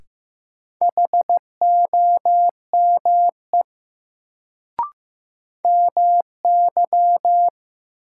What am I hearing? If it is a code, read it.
Morse code, heard as a keyed tone, about 15 wpm, 700 Hz. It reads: HOME MY
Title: Sentences from Top 200 Words - New Word Drill - 15wpm